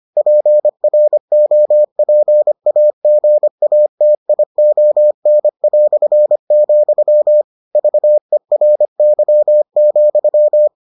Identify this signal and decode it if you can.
Morse code, heard as a keyed tone, about 25 wpm, 600 Hz. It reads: PROPAGATION", VERY,